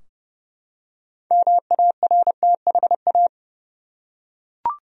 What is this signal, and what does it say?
Morse code, heard as a keyed tone, about 31 wpm, 700 Hz. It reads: MARTHA
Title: USA Common Names - 30wpm